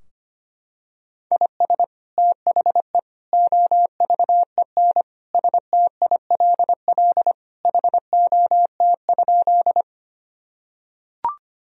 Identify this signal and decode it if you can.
Morse code, heard as a keyed tone, about 25 wpm, 700 Hz. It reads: IS THE OVEN STILL HOT?
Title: Sentences from Top 1000 Words - Review Part 4 - 25wpm